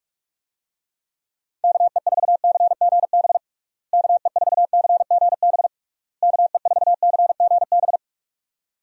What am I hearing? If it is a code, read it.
Morse code, heard as a keyed tone, about 41 wpm, 700 Hz. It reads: KE4CGB KE4CGB KE4CGB
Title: USA Call Signs - 6 Characters - Repeated 3x - 45wpm